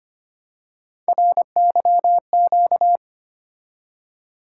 Morse code, heard as a keyed tone, about 25 wpm, 700 Hz. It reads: RYQ
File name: Three Letter-Number Combinations - 25wpm